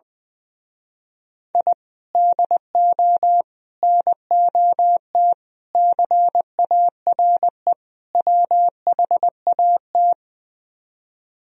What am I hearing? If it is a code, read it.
Morse code, heard as a keyed tone, about 20 wpm, 700 Hz. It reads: I DO NOT CARE WHAT
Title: Speed-Racing - Sets of 5 Words 20wpm